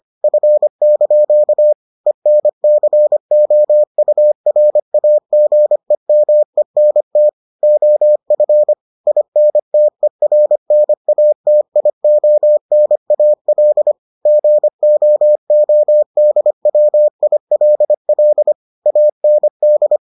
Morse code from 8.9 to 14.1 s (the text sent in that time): INTERNATIONAL